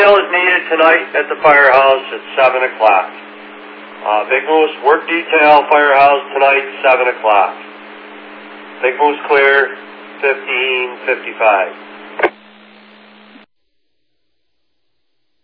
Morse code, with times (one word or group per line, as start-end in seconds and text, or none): none